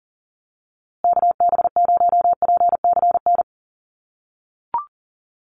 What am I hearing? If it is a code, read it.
Morse code, heard as a keyed tone, about 40 wpm, 700 Hz. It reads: KB0PCN
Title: USA Call Signs Repeated 2x 40wpm